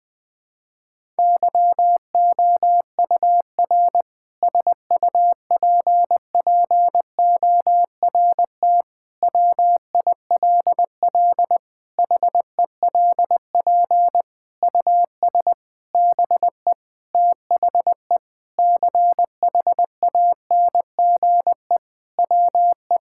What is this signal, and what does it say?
Morse code, heard as a keyed tone, about 20 wpm, 700 Hz. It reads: YOUR SUPPORT WILL HELP US BE THE CHANGE WE
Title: Sentences with 7-Word Phrases - 20wpm